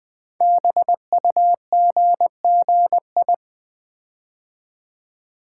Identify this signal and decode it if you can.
Morse code, heard as a keyed tone, about 20 wpm, 700 Hz. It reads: BUGGI
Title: Code Groups - 5 Letter:Numbers - 20wpm